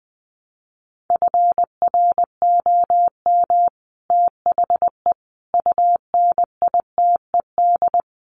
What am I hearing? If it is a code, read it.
Morse code, heard as a keyed tone, about 20 wpm, 700 Hz. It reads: FROM THE UNITED
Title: Sets of 4 Words 20wpm